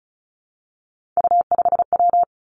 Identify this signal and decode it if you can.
Morse code, heard as a keyed tone, about 36 wpm, 700 Hz. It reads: U5W
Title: Three Challenging Letter-Number Combinations 35wpm